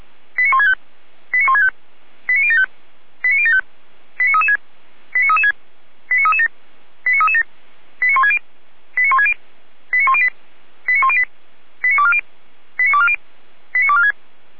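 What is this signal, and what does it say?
Five-tone selective call (ZVEI), heard as quick runs of five tones, tones of about 70 ms. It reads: 89165 89165 89075 89075 89336 89336 89337 89337 89160 89160 89108 89108 89240 89240 89246
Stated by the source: ZVEI five-tone selective call 89072 received via SDR.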